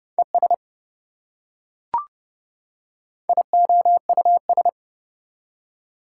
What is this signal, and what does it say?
Morse code, heard as a keyed tone, about 28 wpm, 700 Hz. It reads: ES IOUS